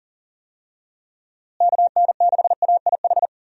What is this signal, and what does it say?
Morse code, heard as a keyed tone, about 40 wpm, 700 Hz. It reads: KN6AIH
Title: USA Call Signs - 6 Characters - Repeated 2x - 40wpm